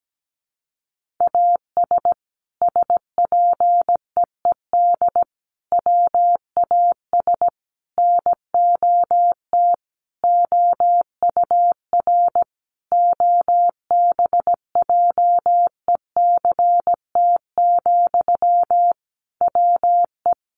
Morse code, heard as a keyed tone, about 17 wpm, 700 Hz. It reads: AS SPEED WAS NOT OUR OBJECT, WE
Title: Sentences from Top 800 Words 17wpm